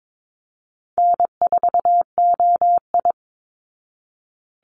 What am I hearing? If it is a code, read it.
Morse code, heard as a keyed tone, about 22 wpm, 700 Hz. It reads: N4OI
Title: Easy Call Signs Repeated 3x - 22wpm